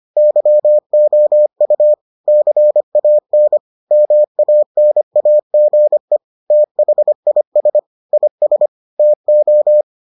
Morse code, heard as a keyed tone, about 25 wpm, 600 Hz. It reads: YOU CAN MANAGE THIS IS TO